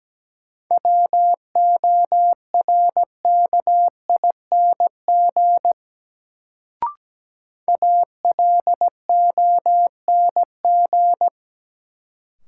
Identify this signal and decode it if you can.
Morse code, heard as a keyed tone, about 17 wpm, 700 Hz. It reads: WORKING ALONG
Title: Sentences from Top 700 Words - New Word Drill - 17wpm